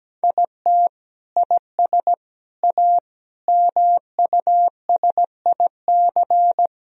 Morse code, heard as a keyed tone, about 17 wpm, 700 Hz. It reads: IT IS A MUSIC
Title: Sentences from Top 1000 Words - Review Part 4 - 17wpm